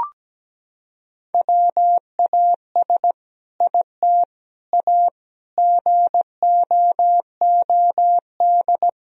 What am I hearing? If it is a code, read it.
Morse code, heard as a keyed tone, about 17 wpm, 700 Hz. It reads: WAS IT A GOOD
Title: Sentences from Top 1000 Words - Review Part 4 - 17wpm